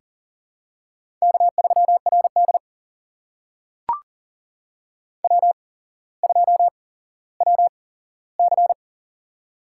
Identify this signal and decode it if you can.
Morse code, heard as a keyed tone, about 40 wpm, spaced as for 17 wpm, 700 Hz. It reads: K3RD W2WC
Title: USA Call Signs - 4 Characters - 8x Character Spacing - 40wpm